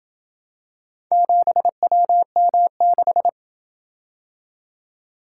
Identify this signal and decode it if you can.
Morse code, heard as a keyed tone, about 28 wpm, 700 Hz. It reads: 7WM6